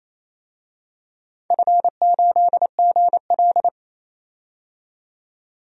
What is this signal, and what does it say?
Morse code, heard as a keyed tone, about 28 wpm, 700 Hz. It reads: F8GL
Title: DXCC-100 Call Signs - 4 Characters - 28wpm